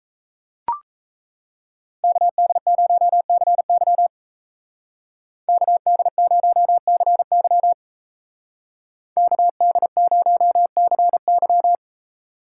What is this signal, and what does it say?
Morse code, heard as a keyed tone, about 37 wpm, 700 Hz. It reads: KD0CY KD0CY KD0CY